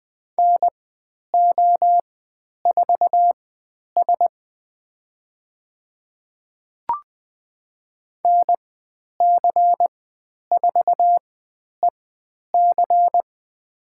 Morse code, heard as a keyed tone, about 20 wpm, spaced as for 10 wpm, 700 Hz. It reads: NO4S NC4EC
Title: 102 Lesson 17g - Letter u - Call Signs 20wpm